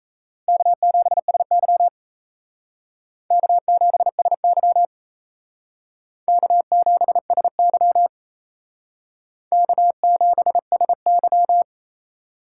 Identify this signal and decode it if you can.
Morse code, heard as a keyed tone, about 35 wpm, 700 Hz. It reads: K7SY K7SY K7SY K7SY